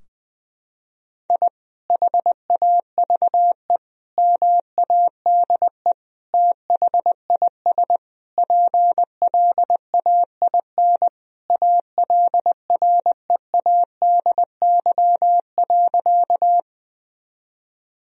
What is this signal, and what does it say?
Morse code, heard as a keyed tone, about 20 wpm, 700 Hz. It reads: I HAVE MADE THIS PLAIN ALREADY.
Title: Sentences from Top 1000 Words - Review Part 2 - 20wpm